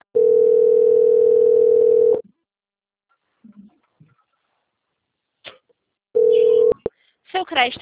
Ringback tone (North America), 440 + 480 Hz.
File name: out- - - - - .20478